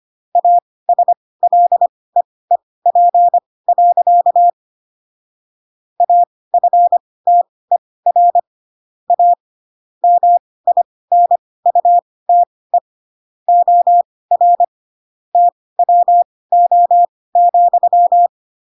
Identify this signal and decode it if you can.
Morse code, heard as a keyed tone, about 25 wpm, spaced as for 18 wpm, 700 Hz. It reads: ASLEEP. AFTER A MINUTE OR TWO,
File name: Pinocchio / Pinocchio ch6 12 25 ik1hge